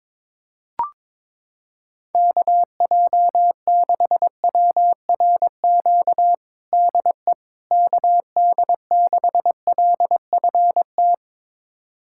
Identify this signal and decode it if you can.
Morse code, heard as a keyed tone, about 22 wpm, 700 Hz. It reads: KJ6WRQ DE KD6LFT
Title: USA Callsign DE Callsign - 6 Characters - 22wpm